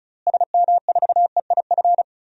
Morse code, heard as a keyed tone, about 36 wpm, 700 Hz. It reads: SM4EIF
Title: DXCC-100 Call Signs - 6 Characters - Repeated 3x - 35wpm